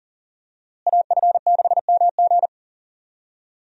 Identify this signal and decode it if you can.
Morse code, heard as a keyed tone, about 40 wpm, 700 Hz. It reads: AF6MG
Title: USA Call Signs - 5 Characters - Repeated 3x - 40wpm